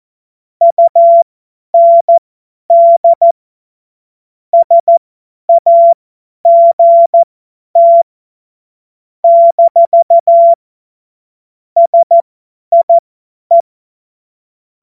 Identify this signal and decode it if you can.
Morse code, heard as a keyed tone, about 14 wpm, spaced as for 10 wpm, 680 Hz. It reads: UND SAGT - SIE
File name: Morsekurs / Witze / Witz 20. Januar 2026 50 F70